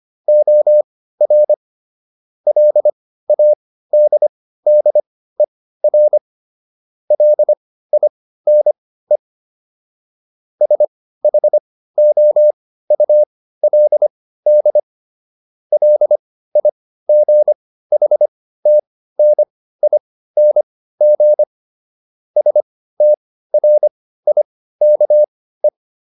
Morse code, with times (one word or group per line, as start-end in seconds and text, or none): 0.3-1.5 OR
2.5-6.2 LADDER
7.1-9.2 LINE
10.6-14.8 SHOULD
15.7-21.4 LIGHTNING
22.4-25.7 STRIKE